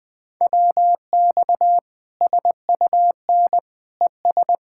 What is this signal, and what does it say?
Morse code, heard as a keyed tone, about 20 wpm, 700 Hz. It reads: WX SUN ES
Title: Practice QSO - Exchange 2 - Info - 20wpm